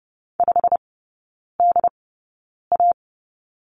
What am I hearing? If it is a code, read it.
Morse code, heard as a keyed tone, about 30 wpm, spaced as for 9 wpm, 700 Hz. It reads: 5DA
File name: course / 30:10wpm / 087 Lesson 15e - Number 5 - Three Characters 30wpm